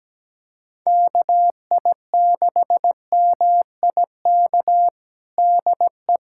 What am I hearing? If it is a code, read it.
Morse code, heard as a keyed tone, about 17 wpm, 700 Hz. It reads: KI6MIK DE
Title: USA Callsign DE Callsign - 6 Characters - 17wpm